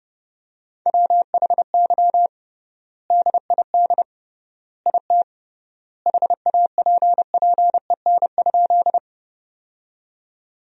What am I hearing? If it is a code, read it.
Morse code, heard as a keyed tone, about 31 wpm, 700 Hz. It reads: WHY DID IT HAPPEN?
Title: 3x Word-Spacing - Sentences from Top 400 Words - 30wpm